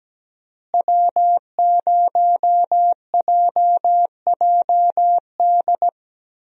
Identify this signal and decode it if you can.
Morse code, heard as a keyed tone, about 17 wpm, 700 Hz. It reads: W0JJD
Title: USA Call Signs - 5 Characters - Repeated 3x - 17wpm